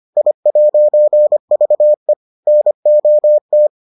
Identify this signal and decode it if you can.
Morse code, heard as a keyed tone, about 25 wpm, 600 Hz. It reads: I'VE NOT